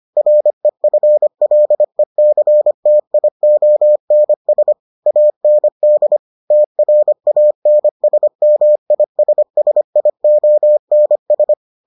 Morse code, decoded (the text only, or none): REFLECTIONS AND TRANSMISSIONS